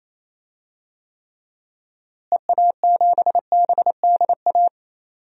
Morse code, heard as a keyed tone, about 28 wpm, 700 Hz. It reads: EA7BDA
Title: DXCC-100 Call Signs - 6 Characters - Repeated 2x - 28wpm